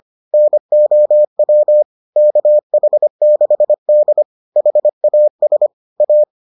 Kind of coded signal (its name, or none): Morse code